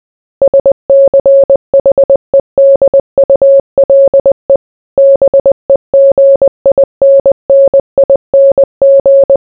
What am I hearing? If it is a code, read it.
Morse code, heard as a keyed tone, about 20 wpm, 550 Hz. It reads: SCHEDULE BEGINNING